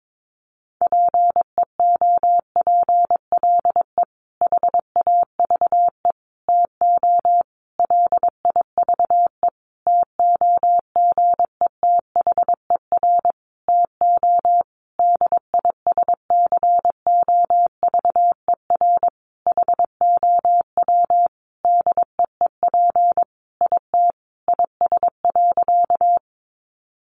Morse code, heard as a keyed tone, about 22 wpm, 700 Hz. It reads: PEOPLE HAVE TO LIVE TOGETHER TO DISCOVER HOW DEEP IT IS.
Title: Sentences from Top 1100 Words 22wpm